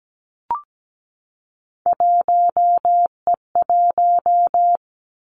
Morse code, heard as a keyed tone, about 17 wpm, 700 Hz. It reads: 1E1
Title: Three Challenging Letter-Number Combinations 17wpm